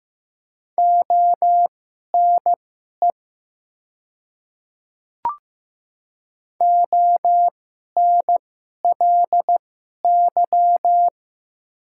Morse code, heard as a keyed tone, about 15 wpm, spaced as for 11 wpm, 700 Hz. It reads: ONE ONLY
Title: 2x Character Spacing - Top 200 Words - 15wpm